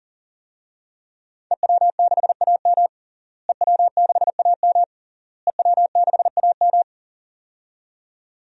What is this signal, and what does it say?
Morse code, heard as a keyed tone, about 37 wpm, 700 Hz. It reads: EW6AM EW6AM EW6AM
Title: DXCC-100 Call Signs - 5 Characters - Repeated 3x - 40wpm